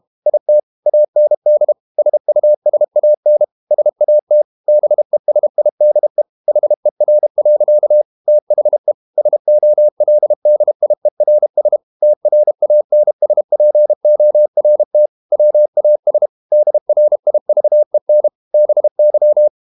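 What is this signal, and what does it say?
Morse code, heard as a keyed tone, about 32 wpm, 600 Hz. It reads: IT AND SUSAN SAT BESIDE HER. THE SOLDIERS TRANSPORT WAS DRIVEN BY
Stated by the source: Morse code practice at 32 wpm from CW med Gnister, reading Ken Follett's novel Never in English.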